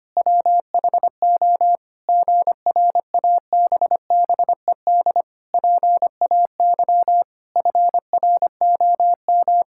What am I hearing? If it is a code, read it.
Morse code, heard as a keyed tone, about 25 wpm, 700 Hz. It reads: WHO GRABBED PAY FROM
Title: Pangram Challenge v2 - 25wpm